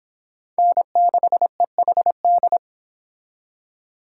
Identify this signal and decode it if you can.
Morse code, heard as a keyed tone, about 29 wpm, 700 Hz. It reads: N6EHD